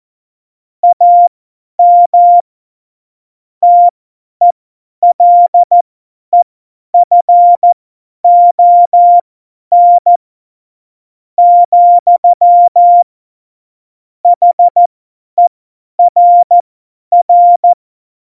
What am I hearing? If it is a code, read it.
Morse code, heard as a keyed tone, about 14 wpm, spaced as for 9 wpm, 700 Hz. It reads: AM TELEFON, HERR
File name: Morsekurs / Witze / Witz 06 March 2018, 50 F70